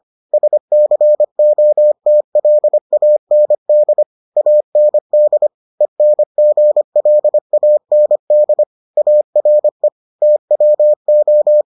Morse code, heard as a keyed tone, about 25 wpm, 600 Hz. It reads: SCOTLAND AND ENGLAND ARE TWO